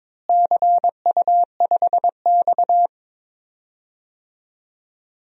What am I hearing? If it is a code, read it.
Morse code, heard as a keyed tone, about 22 wpm, 700 Hz. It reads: CU5X
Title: DXCC-100 Call Signs - 4 Characters - Repeated 2x - 22wpm